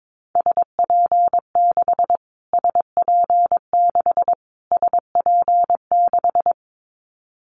Morse code, heard as a keyed tone, about 22 wpm, 700 Hz. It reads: SP6 SP6 SP6
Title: Callsign Prefixes for Top 25 DXCC - Repeated 3x - 22wpm